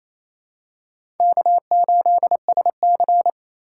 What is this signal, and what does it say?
Morse code, heard as a keyed tone, about 27 wpm, 700 Hz. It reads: K8SC
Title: USA Call Signs - 4 Characters - 4x Character Spacing - 28wpm